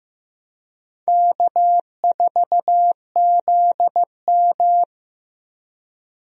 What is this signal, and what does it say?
Morse code, heard as a keyed tone, about 15 wpm, 700 Hz. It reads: K4ZM
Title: USA Call Signs - 4 Characters - 15wpm